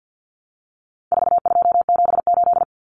Morse code, heard as a keyed tone, about 49 wpm, 700 Hz. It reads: 4278
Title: Numbers - 4-Digit 50wpm